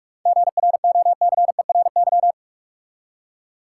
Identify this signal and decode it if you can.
Morse code, heard as a keyed tone, about 42 wpm, 700 Hz. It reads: GROCERY
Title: Seven Letter Words - 45wpm